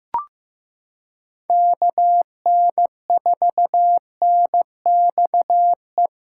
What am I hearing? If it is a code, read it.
Morse code, heard as a keyed tone, about 15 wpm, 700 Hz. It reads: KN4NXE
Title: USA Callsign DE Callsign 15wpm